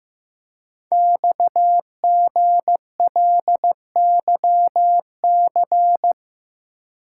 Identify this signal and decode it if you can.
Morse code, heard as a keyed tone, about 15 wpm, 700 Hz. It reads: XGLYC
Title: Code Groups - 5 Letters - 15wpm